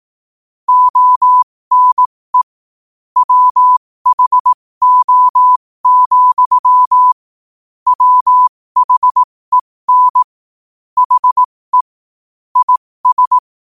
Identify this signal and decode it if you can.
Morse code, heard as a keyed tone, about 18 wpm, spaced as for 15 wpm, 1000 Hz. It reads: ONE WHO, WHEN HE IS
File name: qotdfiles / 15wpm